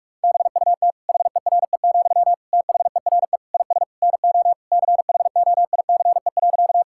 Morse code, heard as a keyed tone, about 46 wpm, 700 Hz. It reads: BUT HERE, THERE IS NO CHOICE.
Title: Sentences from Top 1000 Words - Review Part 6 - 45wpm